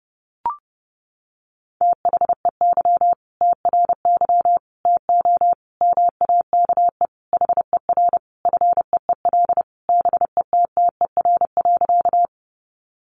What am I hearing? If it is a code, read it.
Morse code, heard as a keyed tone, about 30 wpm, 700 Hz. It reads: THEY TRY TO MAKE HER FEEL BETTER.
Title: Easy Phrases 30wpm